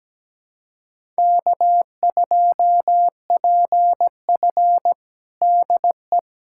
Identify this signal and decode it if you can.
Morse code, heard as a keyed tone, about 17 wpm, 700 Hz. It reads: K2PF DE
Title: USA Callsign DE Callsign - 4 Characters - 17wpm